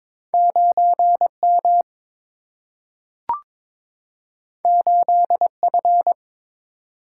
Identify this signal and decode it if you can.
Morse code, heard as a keyed tone, about 21 wpm, 700 Hz. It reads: 9 M 8 F
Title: Single-Letter Single-Number 22wpm